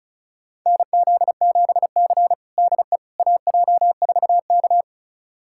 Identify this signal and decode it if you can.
Morse code, heard as a keyed tone, about 34 wpm, 700 Hz. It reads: NZ7C DE AJ4K
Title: USA Callsign DE Callsign - 4 Characters - 35wpm